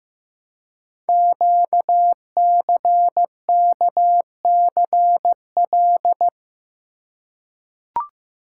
Morse code, heard as a keyed tone, about 15 wpm, 700 Hz. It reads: QCKCL
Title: Code Groups - 5 Letters - 15wpm